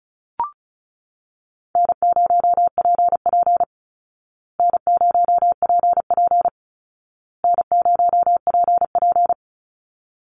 Morse code, heard as a keyed tone, about 34 wpm, 700 Hz. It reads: N0PP N0PP N0PP
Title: Call Signs Repeated 2x - 35wpm